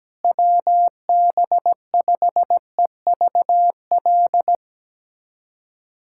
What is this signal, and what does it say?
Morse code, heard as a keyed tone, about 17 wpm, 700 Hz. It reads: WB5EVL
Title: USA Callsign DE Callsign 17wpm